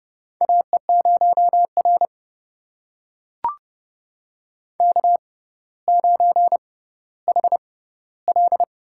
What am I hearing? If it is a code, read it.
Morse code, heard as a keyed tone, about 30 wpm, spaced as for 15 wpm, 700 Hz. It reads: AE0R K9HL
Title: USA Call Signs - 4 Characters - 6x Character Spacing - 30wpm